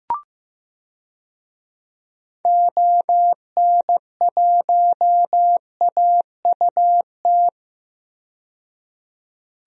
Morse code, heard as a keyed tone, about 15 wpm, 700 Hz. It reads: ON1AUT
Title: DXCC-100 Call Signs - 6 Characters - 15wpm